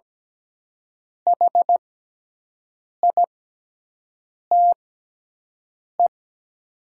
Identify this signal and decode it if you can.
Morse code, heard as a keyed tone, about 17 wpm, spaced as for 6 wpm, 700 Hz. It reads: HITE
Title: 6x Character Spacing - Top 300 Words - 17wpm